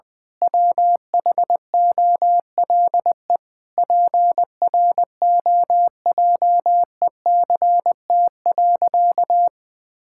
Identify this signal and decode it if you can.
Morse code, heard as a keyed tone, about 20 wpm, 700 Hz. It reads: WHOLE PROJECT.
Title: Sentences with 7-Word Phrases - 20wpm